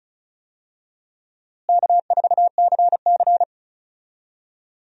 Morse code, heard as a keyed tone, about 36 wpm, 700 Hz. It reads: K4CC
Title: USA Call Signs - 4 Characters - Repeated 3x - 35wpm